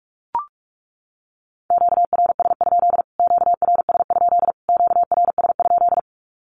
Morse code, heard as a keyed tone, about 44 wpm, 700 Hz. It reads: QRS? QRS? QRS?
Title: Q-Codes Repeated 3x 45wpm